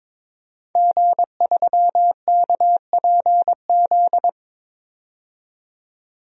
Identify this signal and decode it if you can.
Morse code, heard as a keyed tone, about 22 wpm, 700 Hz. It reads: G3KPZ
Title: DXCC-100 Call Signs - 5 Characters - 22wpm